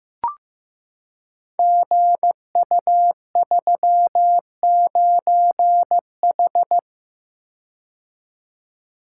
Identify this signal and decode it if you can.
Morse code, heard as a keyed tone, about 15 wpm, 700 Hz. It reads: GU39H
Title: Code Groups - 5 Letter:Numbers - 15wpm